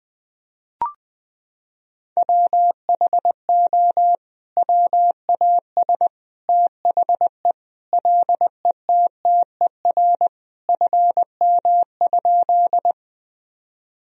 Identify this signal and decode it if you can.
Morse code, heard as a keyed tone, about 21 wpm, 700 Hz. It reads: WHO WAS THE LETTER FM?
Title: Sentences from Top 300 Words and Abbreviations 20wpm